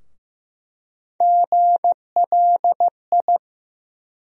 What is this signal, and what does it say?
Morse code, heard as a keyed tone, about 15 wpm, 700 Hz. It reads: GLI
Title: USA Call Sign Postfixes - 3 Letters 15wpm